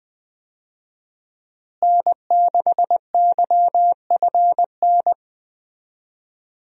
Morse code, heard as a keyed tone, about 20 wpm, 700 Hz. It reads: N6YFN
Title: USA Call Signs - 5 Characters - Repeated 3x - 20wpm